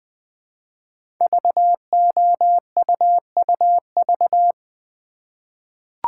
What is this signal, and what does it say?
Morse code, heard as a keyed tone, about 20 wpm, 700 Hz. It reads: VOUUV
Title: Code Groups - 5 Letters - 20wpm